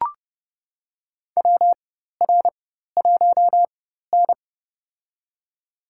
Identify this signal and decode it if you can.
Morse code, heard as a keyed tone, about 30 wpm, spaced as for 14 wpm, 690 Hz. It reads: WR1N
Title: USA Call Signs - 4x Character Spacing - 30wpm